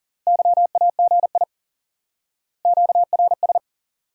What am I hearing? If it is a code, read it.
Morse code, heard as a keyed tone, about 37 wpm, 700 Hz. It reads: YAGI QRS
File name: IQE Course 3 - Lesson 89 - YAGI - Two Elements - 40wpm